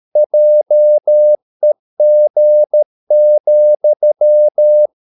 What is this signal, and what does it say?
Morse code, heard as a keyed tone, about 13 wpm, 600 Hz. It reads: JEG,
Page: Ferie 13 wpm